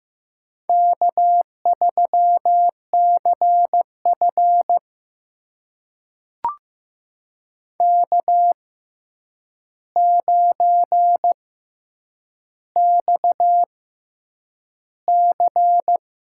Morse code, heard as a keyed tone, about 15 wpm, spaced as for 8 wpm, 700 Hz. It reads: K3CF K9XC
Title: USA Call Signs - 4 Characters - 6x Character Spacing - 15wpm